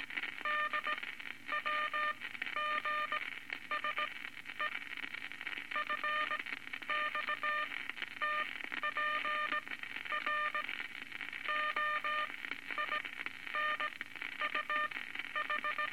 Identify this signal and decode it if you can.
Morse code, heard as a keyed tone, about 18 wpm, spaced as for 11 wpm, 1210 Hz, with noise. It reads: DWGSE FXTPR OINUH
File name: Morse Code Records / 19 A LCM 166 Ten Words Per Minute